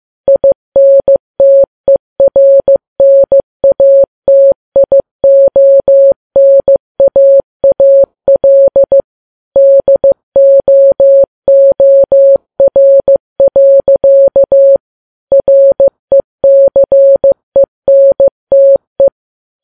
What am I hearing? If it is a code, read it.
Morse code, heard as a keyed tone, about 15 wpm, 550 Hz. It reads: INTERNATIONAAL DOOR. RECENTE
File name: Files Web tls / Morse on9fv / morse15kps